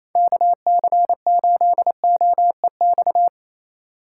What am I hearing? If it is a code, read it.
Morse code, heard as a keyed tone, about 28 wpm, 700 Hz. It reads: KC8OEX
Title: USA Call Signs - 6 Characters - 8x Character Spacing - 28wpm